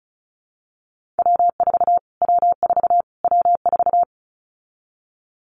Morse code, heard as a keyed tone, about 34 wpm, 700 Hz. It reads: W4 W4 W4
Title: Callsign Prefixes for Top 25 DXCC - Repeated 3x - 35wpm